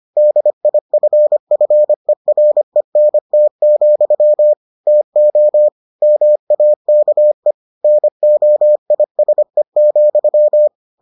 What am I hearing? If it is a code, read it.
Morse code, heard as a keyed tone, about 25 wpm, 600 Hz. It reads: DIFFERENT, TO MAKE NOISE,